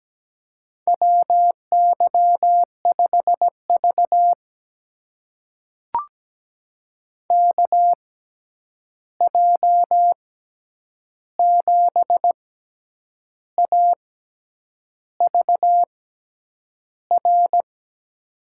Morse code, heard as a keyed tone, about 17 wpm, spaced as for 8 wpm, 700 Hz. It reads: WY5V KJ7AVR
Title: USA Call Signs - 6x Character Spacing - 17wpm